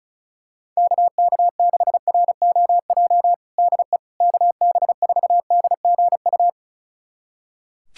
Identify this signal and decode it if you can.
Morse code, heard as a keyed tone, about 36 wpm, 700 Hz. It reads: KK6ROJ DE KB4DGU
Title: USA Callsign DE Callsign - 6 Characters - 35wpm